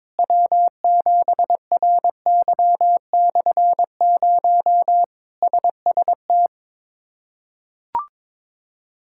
Morse code, heard as a keyed tone, about 22 wpm, 700 Hz. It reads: W7RY/0 SST
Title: K1USN SST QSO Practice - 22wpm